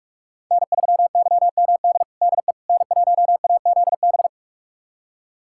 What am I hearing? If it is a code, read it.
Morse code, heard as a keyed tone, about 46 wpm, 700 Hz. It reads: N3YMD DE N1AZB
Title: USA Callsign DE Callsign - 5 Characters - 45wpm